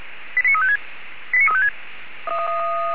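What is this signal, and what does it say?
Five-tone selective call (ZVEI), heard as quick runs of five tones, tones of about 80 ms. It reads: 89356
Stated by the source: Recorded via SDR by feuerwehr-orlamuende.org (fire brigade Orlamünde), ZVEI five-tone call 89357.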